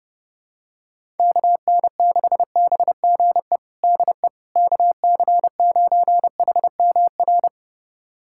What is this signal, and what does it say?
Morse code, heard as a keyed tone, about 30 wpm, 700 Hz. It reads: KN6BGE DE KC9HMR
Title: USA Callsign DE Callsign 30wpm